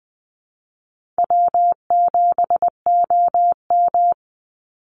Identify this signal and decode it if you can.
Morse code, heard as a keyed tone, about 20 wpm, 700 Hz. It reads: W7OM
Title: Easy Call Signs Repeated 2x - 20wpm